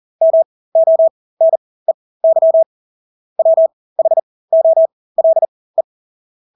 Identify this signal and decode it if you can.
Morse code, heard as a keyed tone, about 40 wpm, spaced as for 20 wpm, 650 Hz. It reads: MONEY WHOLE
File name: Engelske ord 40-20 wpm